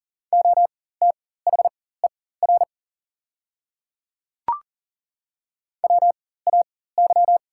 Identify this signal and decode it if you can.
Morse code, heard as a keyed tone, about 40 wpm, spaced as for 19 wpm, 690 Hz. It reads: OTHER WAY
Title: 4x Character Spacing - Top 100 Words - 40wpm